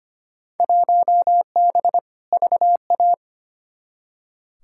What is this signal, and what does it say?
Morse code, heard as a keyed tone, about 25 wpm, 700 Hz. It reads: 1B VA
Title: Field Day Exchanges - 25wpm